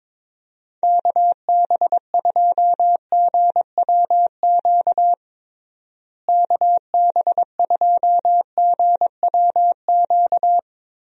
Morse code, heard as a keyed tone, about 22 wpm, 700 Hz. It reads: KB2GWQ KB2GWQ
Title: USA Call Signs - 6 Characters - Repeated 3x - 22wpm